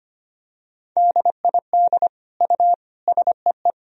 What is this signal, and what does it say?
Morse code, heard as a keyed tone, about 25 wpm, 700 Hz. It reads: DID U SEE
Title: Sentences from Top 100 Words and Abbreviations 25wpm